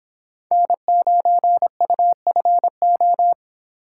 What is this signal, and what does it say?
Morse code, heard as a keyed tone, about 24 wpm, 700 Hz. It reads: N9UFO